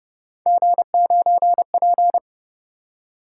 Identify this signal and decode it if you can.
Morse code, heard as a keyed tone, about 31 wpm, 700 Hz. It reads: G9P